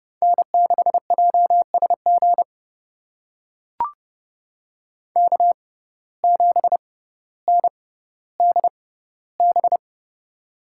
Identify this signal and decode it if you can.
Morse code, heard as a keyed tone, about 30 wpm, spaced as for 15 wpm, 700 Hz. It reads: N6JSG K7NDB
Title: USA Call Signs - 5 Characters - 6x Character Spacing - 30wpm